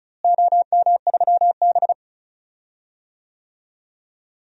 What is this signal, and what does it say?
Morse code, heard as a keyed tone, about 34 wpm, 700 Hz. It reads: OM3B
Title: DXCC-100 Call Signs - 4 Characters - Repeated 3x - 35wpm